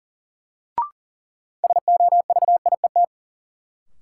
Morse code, heard as a keyed tone, about 40 wpm, 700 Hz. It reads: SOVIET
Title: Sets of 5 Words Encore - New Word Drill - 40wpm